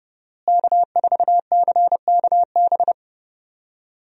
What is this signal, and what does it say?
Morse code, heard as a keyed tone, about 30 wpm, 700 Hz. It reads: K4CKB
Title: USA Call Signs - 5 Characters - Repeated 3x - 30wpm